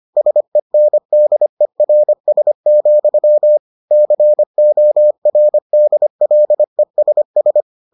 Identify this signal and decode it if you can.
Morse code, heard as a keyed tone, about 25 wpm, 600 Hz. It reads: SENDERS, CORDLESS